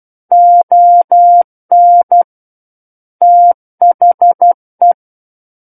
Morse code, heard as a keyed tone, about 12 wpm, 700 Hz. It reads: ON THE
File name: Book One / 03-Chapter 3 - 12 WPM